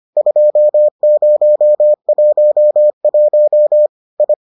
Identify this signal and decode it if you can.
Morse code, heard as a keyed tone, about 25 wpm, 600 Hz. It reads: 2011 I